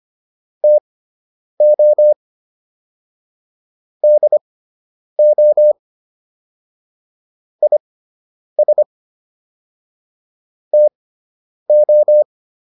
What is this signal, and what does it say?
Morse code, heard as a keyed tone, about 25 wpm, spaced as for 9 wpm, 600 Hz. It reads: TO DO IS TO